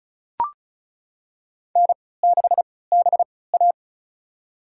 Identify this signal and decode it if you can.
Morse code, heard as a keyed tone, about 36 wpm, spaced as for 20 wpm, 700 Hz. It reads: N6BA
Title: USA Call Signs - 4 Characters - 3x Character Spacing - 35wpm